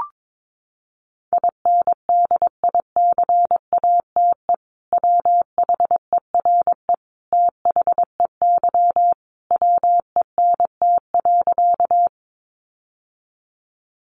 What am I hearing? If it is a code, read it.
Morse code, heard as a keyed tone, about 22 wpm, 700 Hz. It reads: INDICATE WHERE THEY WENT.
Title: Sentences from Top 1300 Words 22wpm